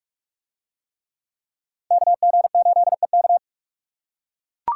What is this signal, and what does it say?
Morse code, heard as a keyed tone, about 39 wpm, 700 Hz. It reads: KG8EK